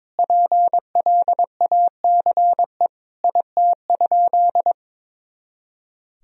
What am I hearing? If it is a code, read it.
Morse code, heard as a keyed tone, about 22 wpm, 700 Hz. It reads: PLACE IT?
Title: Sentences from Top 200 Words and Abbreviations 22wpm